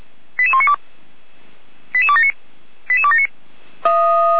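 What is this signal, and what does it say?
Five-tone selective call (ZVEI), heard as quick runs of five tones, tones of about 70 ms. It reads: 88192 88279 88279
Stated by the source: ZVEI five-tone call received by SDR, from the Feuerwehr Orlamünde audio server.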